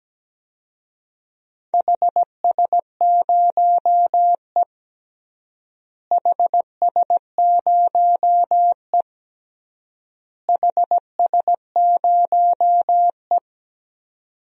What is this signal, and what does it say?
Morse code, heard as a keyed tone, about 17 wpm, 700 Hz. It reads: HS0E HS0E HS0E
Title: DXCC-100 Call Signs - 4 Characters - Repeated 3x - 17wpm